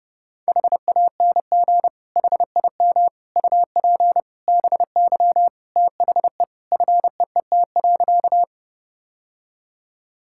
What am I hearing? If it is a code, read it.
Morse code, heard as a keyed tone, about 30 wpm, 700 Hz. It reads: HANG HIM UP BY THE FEET.
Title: Sentences from Top 1600 Words 30wpm